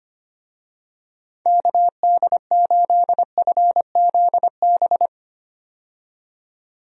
Morse code, heard as a keyed tone, about 25 wpm, 700 Hz. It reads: KD8FZB
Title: USA Call Signs - 6 Characters - 25wpm